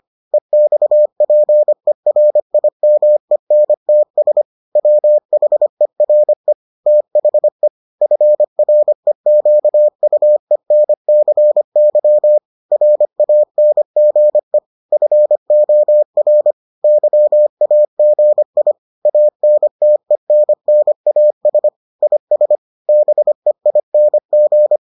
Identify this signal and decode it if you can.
Morse code, heard as a keyed tone, about 25 wpm, 600 Hz. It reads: EXPERIMENTS WHERE THE FREQUENCY RANGE FOR YAGI ANTENNAS IS BEING